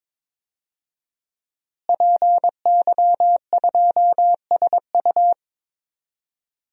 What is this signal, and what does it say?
Morse code, heard as a keyed tone, about 21 wpm, 700 Hz. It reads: PY2SU
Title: DXCC-100 Call Signs - 5 Characters - Repeated 2x - 22wpm